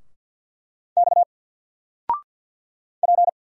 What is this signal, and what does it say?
Morse code, heard as a keyed tone, about 48 wpm, 700 Hz. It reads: X P
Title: Single Letter-Number 50wpm